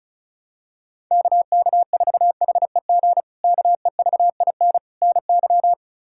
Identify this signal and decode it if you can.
Morse code, heard as a keyed tone, about 36 wpm, 700 Hz. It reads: KK4HEG KEVIN NY
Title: K1USN SST QSO Practice - 35wpm